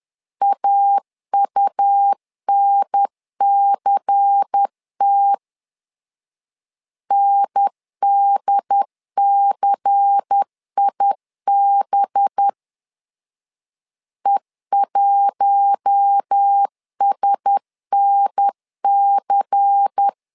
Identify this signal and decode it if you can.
Morse code, heard as a keyed tone, about 10 wpm, 800 Hz. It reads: AUNCT NDCIB E1SNC
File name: LEZIONI / 5 LBVC / 5 LBVC 10wpm 2349